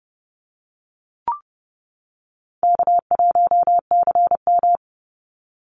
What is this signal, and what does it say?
Morse code, heard as a keyed tone, about 30 wpm, 700 Hz. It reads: K1CM
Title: NA SOTA Chaser Callsigns 30wpm